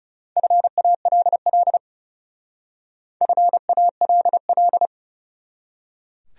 Morse code, heard as a keyed tone, about 36 wpm, 700 Hz. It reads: FALL FALL
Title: Speed-Racing Top 500 Words 30wpm